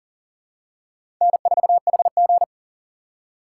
Morse code, heard as a keyed tone, about 40 wpm, 700 Hz. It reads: N4HG
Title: USA Call Signs - 4 Characters - 6x Character Spacing - 40wpm